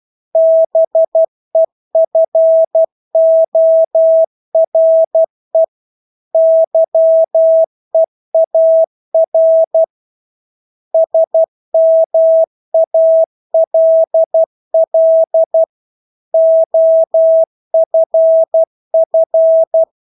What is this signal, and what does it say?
Morse code, heard as a keyed tone, about 12 wpm, 650 Hz. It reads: BEFORE YEAR SMALL OFF
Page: Engelske ord 12 wpm